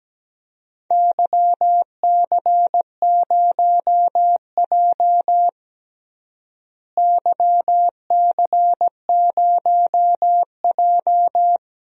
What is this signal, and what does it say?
Morse code, heard as a keyed tone, about 17 wpm, 700 Hz. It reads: YC0J YC0J
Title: DXCC-100 Call Signs - 4 Characters - Repeated 3x - 17wpm